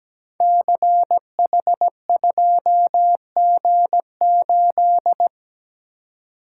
Morse code, heard as a keyed tone, about 17 wpm, 700 Hz. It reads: CH2G8
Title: Code Groups - 5 Letter:Numbers - 17wpm